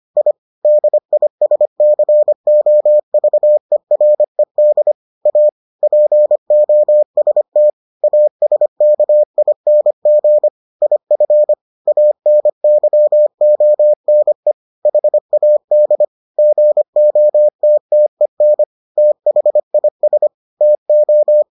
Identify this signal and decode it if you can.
Morse code, heard as a keyed tone, about 25 wpm, 600 Hz. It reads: I DISCOVERED A POST ASKING IF ANYONE HAD GOTTEN THIS TO